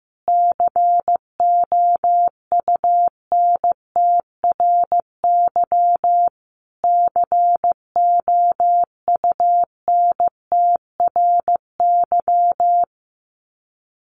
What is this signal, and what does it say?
Morse code, heard as a keyed tone, about 14 wpm, 700 Hz. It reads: COUNTRY COUNTRY
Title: Top 500 Words Repeated 15wpm